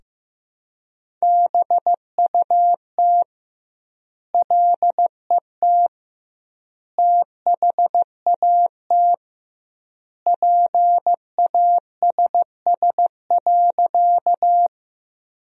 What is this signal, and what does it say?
Morse code, heard as a keyed tone, about 15 wpm, 700 Hz. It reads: BUT LET THAT PASS.
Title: 2x Word-Spacing - Sentences from Top 400 Words - 15wpm